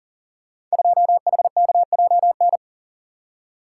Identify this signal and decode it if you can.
Morse code, heard as a keyed tone, about 40 wpm, 700 Hz. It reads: 2HKJN
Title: Code Groups - 5 Letter:Numbers - 40wpm